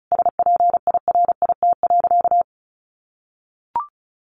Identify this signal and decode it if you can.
Morse code, heard as a keyed tone, about 34 wpm, 700 Hz. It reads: SPIRIT.
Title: Sentences from Top 1400 Words 35wpm